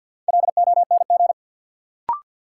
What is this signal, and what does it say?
Morse code, heard as a keyed tone, about 53 wpm, 700 Hz. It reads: LONG
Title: Sentences from Top 100 Words - New Word Drill - 50wpm